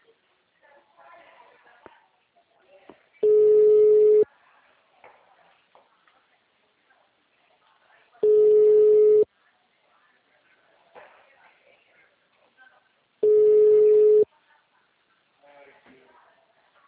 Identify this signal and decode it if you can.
Ringback tone (Europe, 425 Hz), with speech.